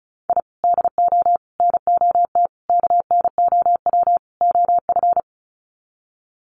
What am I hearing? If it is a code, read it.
Morse code, heard as a keyed tone, about 34 wpm, 700 Hz. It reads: I DO NOT KNOW OF
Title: Sets of 5 Words Encore2 35wpm